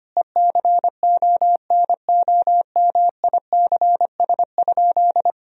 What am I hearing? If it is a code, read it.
Morse code, heard as a keyed tone, about 25 wpm, 700 Hz. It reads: ECONOMICS?
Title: Sentences from Top 1000 Words - Review Part 3 - 25wpm